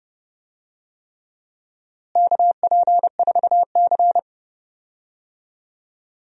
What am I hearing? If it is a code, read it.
Morse code, heard as a keyed tone, about 30 wpm, 700 Hz. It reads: KP4C
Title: DXCC-100 Call Signs - 4 Characters - Repeated 3x - 30wpm